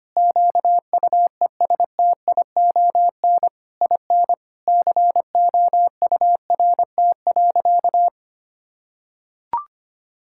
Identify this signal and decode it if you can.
Morse code, heard as a keyed tone, about 25 wpm, 700 Hz. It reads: QUESTION IN COURT.
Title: Sentences from Top 1600 Words 25wpm